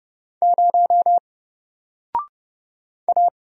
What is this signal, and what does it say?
Morse code, heard as a keyed tone, about 30 wpm, 700 Hz. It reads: 0 A
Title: Single Letter-Number Alt 30wpm